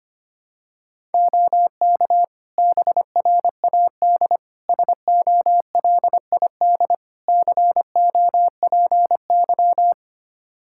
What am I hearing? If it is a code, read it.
Morse code, heard as a keyed tone, about 25 wpm, 700 Hz. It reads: OK BRAD SOLID COPY
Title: Practice QSO - Exchange 3 - More Info - 25wpm